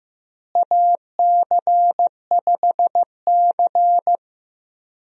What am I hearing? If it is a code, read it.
Morse code, heard as a keyed tone, about 15 wpm, 700 Hz. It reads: AC5C
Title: USA Call Signs - 4 Characters - Repeated 2x - 15wpm